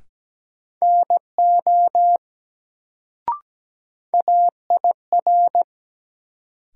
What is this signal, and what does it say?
Morse code, heard as a keyed tone, about 17 wpm, 700 Hz. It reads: NO AIR
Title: Sets of 2 Words - New Word Drill - 17wpm